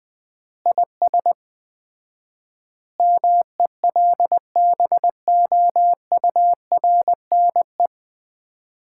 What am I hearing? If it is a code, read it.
Morse code, heard as a keyed tone, about 20 wpm, 700 Hz. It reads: IS MELBOURNE
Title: Common QSO Phrases - 4x Word-Spacing - 20wpm